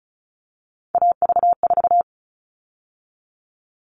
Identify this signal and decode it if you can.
Morse code, heard as a keyed tone, about 34 wpm, 700 Hz. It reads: AV4
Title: All Cut Numbers - 3-Digit 35wpm